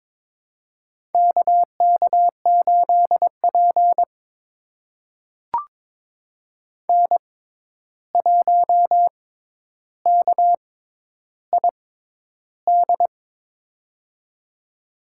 Morse code, heard as a keyed tone, about 21 wpm, spaced as for 10 wpm, 700 Hz. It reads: KK8P N1KID
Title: USA Call Signs - 6x Character Spacing - 22wpm